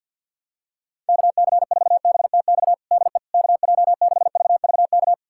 Morse code, heard as a keyed tone, about 52 wpm, 700 Hz. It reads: KC4BTX DE KJ6VVK
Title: USA Callsign DE Callsign - 6 Characters - 50wpm